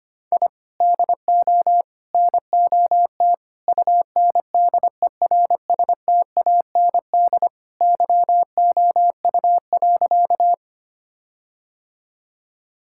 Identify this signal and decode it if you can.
Morse code, heard as a keyed tone, about 25 wpm, 700 Hz. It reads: I DO NOT UNDERSTAND YOU.
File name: practice / 25wpm / Sentences with 4-Word Phrases - 25wpm